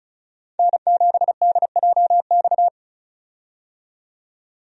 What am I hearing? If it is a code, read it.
Morse code, heard as a keyed tone, about 36 wpm, 700 Hz. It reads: N7DJX
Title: USA Call Signs - 5 Characters - 35wpm